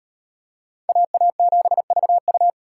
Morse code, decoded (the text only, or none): AA7VU